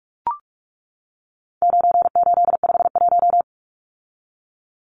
Morse code, heard as a keyed tone, about 44 wpm, 700 Hz. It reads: 9851
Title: Numbers - 4-Digit 45wpm